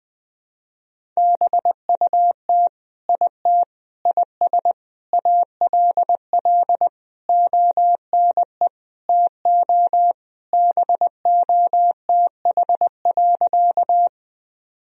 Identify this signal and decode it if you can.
Morse code, heard as a keyed tone, about 19 wpm, 700 Hz. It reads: BUT IT IS ALL ONE TO BOTH.
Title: Sentences from Top 1000 Words - Review Part 1 - 20wpm